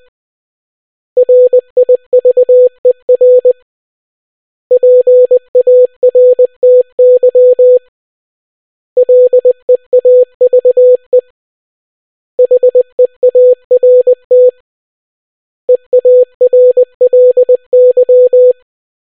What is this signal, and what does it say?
Morse code, heard as a keyed tone, about 20 wpm, 500 Hz. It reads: RIVER PARTY LEAVE HEART EARLY